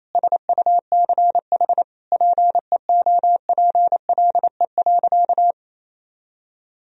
Morse code, heard as a keyed tone, about 28 wpm, 700 Hz. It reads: SUCH PEOPLE.
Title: Sentences from Top 1000 Words - Review Part 3 - 28wpm